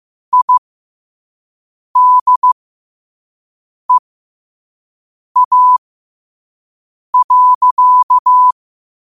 Morse code, heard as a keyed tone, about 15 wpm, spaced as for 5 wpm, 1000 Hz. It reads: IDEA.
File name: qotdfiles / 5wpm